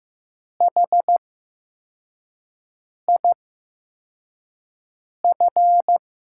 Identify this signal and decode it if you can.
Morse code, heard as a keyed tone, about 15 wpm, spaced as for 4 wpm, 700 Hz. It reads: HIF
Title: USA Call Signs - 5 Characters - 8x Character Spacing - 15wpm